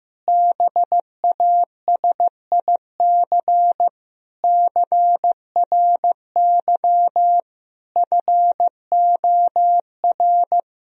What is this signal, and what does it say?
Morse code, heard as a keyed tone, about 15 wpm, 700 Hz. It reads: BASIC CRY FOR
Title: Sentences from Top 1000 Words - Review Part 3 - 15wpm